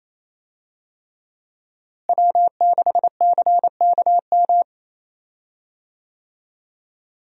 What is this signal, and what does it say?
Morse code, heard as a keyed tone, about 27 wpm, 700 Hz. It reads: W6CKM
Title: USA Call Signs - 5 Characters - Repeated 3x - 28wpm